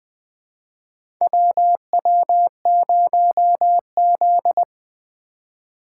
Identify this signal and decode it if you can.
Morse code, heard as a keyed tone, about 21 wpm, 700 Hz. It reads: WW0Z
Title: USA Call Signs - 4 Characters - 3x Character Spacing - 20wpm